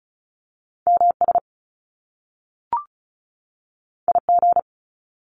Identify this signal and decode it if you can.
Morse code, heard as a keyed tone, about 36 wpm, 700 Hz. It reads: MS IG
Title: Top Two Letter Combinations in Words 35wpm